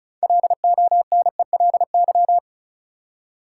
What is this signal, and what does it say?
Morse code, heard as a keyed tone, about 34 wpm, 700 Hz. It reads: LONELY
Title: Six Letter Words - 35wpm